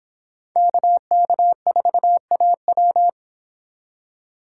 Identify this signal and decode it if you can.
Morse code, heard as a keyed tone, about 26 wpm, 700 Hz. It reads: KK4AW